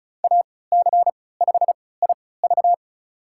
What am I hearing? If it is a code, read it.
Morse code, heard as a keyed tone, about 34 wpm, spaced as for 20 wpm, 700 Hz. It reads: AC5IV
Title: USA Call Signs - 3x Character Spacing - 35wpm